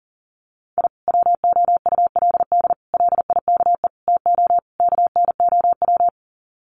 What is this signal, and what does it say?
Morse code, heard as a keyed tone, about 40 wpm, 700 Hz. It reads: I WOULD LIKE TO KNOW
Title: Sets of 5 Words Encore 40wpm